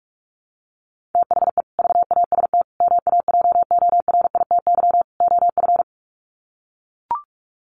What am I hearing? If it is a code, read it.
Morse code, heard as a keyed tone, about 44 wpm, 700 Hz. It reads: THE VAST MAJORITY OF
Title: Sets of 4 Words 45wpm